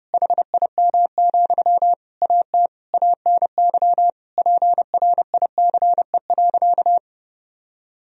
Morse code, heard as a keyed tone, about 30 wpm, 700 Hz. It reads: HIM, AT ANY PRICE.
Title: Sentences from Top 1000 Words - Review Part 4 - 30wpm